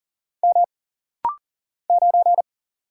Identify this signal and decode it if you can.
Morse code, heard as a keyed tone, about 40 wpm, 700 Hz. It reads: M 9
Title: Single Letter-Number - Rapid-Fire 40wpm